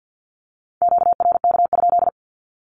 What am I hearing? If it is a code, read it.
Morse code, heard as a keyed tone, about 50 wpm, 700 Hz. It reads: QRK?
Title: Q-Codes 50wpm